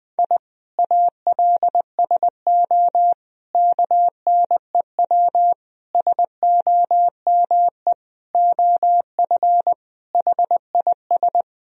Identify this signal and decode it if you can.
Morse code, heard as a keyed tone, about 20 wpm, 700 Hz. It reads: I ALSO KNEW SOME OF HIS
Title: Sentences from Top 1000 Words - Review Part 1 - 20wpm